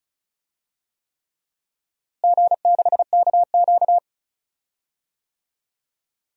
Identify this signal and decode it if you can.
Morse code, heard as a keyed tone, about 34 wpm, 700 Hz. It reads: G6KQ
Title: DXCC-100 Call Signs - 4 Characters - Repeated 3x - 35wpm